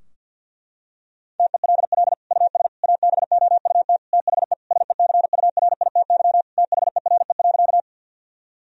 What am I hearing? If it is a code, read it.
Morse code, heard as a keyed tone, about 51 wpm, 700 Hz. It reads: TELL US ABOUT THE SECURITY THERE.
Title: Sentences from Top 1000 Words - Review Part 4 - 50wpm